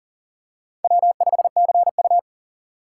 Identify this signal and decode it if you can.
Morse code, heard as a keyed tone, about 40 wpm, 700 Hz. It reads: W5CU
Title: USA Call Signs - 4 Characters - 4x Character Spacing - 40wpm